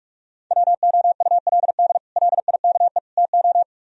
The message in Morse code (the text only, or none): WOULD LIKE TO